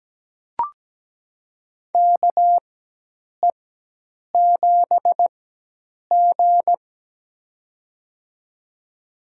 Morse code, heard as a keyed tone, about 17 wpm, spaced as for 8 wpm, 700 Hz. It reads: KE7G
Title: USA Call Signs - 4 Characters - 4x Character Spacing - 17wpm